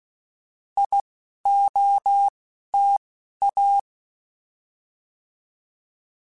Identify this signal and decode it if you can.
Morse code, heard as a keyed tone, about 16 wpm, spaced as for 11 wpm, 800 Hz. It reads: IOTA
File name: Alphabetical / Alphabetical mp3s / Is @10